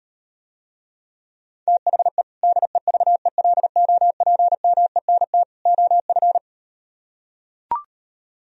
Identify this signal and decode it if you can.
Morse code, heard as a keyed tone, about 38 wpm, 700 Hz. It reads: THE DEVELOPMENT OF